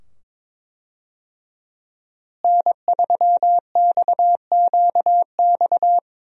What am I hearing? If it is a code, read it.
Morse code, heard as a keyed tone, about 22 wpm, 700 Hz. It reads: N3XQX
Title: USA Call Signs - 5 Characters - Repeated 2x - 22wpm